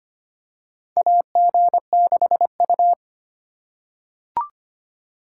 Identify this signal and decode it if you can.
Morse code, heard as a keyed tone, about 25 wpm, 700 Hz. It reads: AG6U
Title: USA Call Signs - 4 Characters - 4x Character Spacing - 25wpm